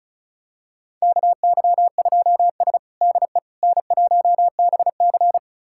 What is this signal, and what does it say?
Morse code, heard as a keyed tone, about 36 wpm, 700 Hz. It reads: KY2S DE N1BC
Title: USA Callsign DE Callsign - 4 Characters - 35wpm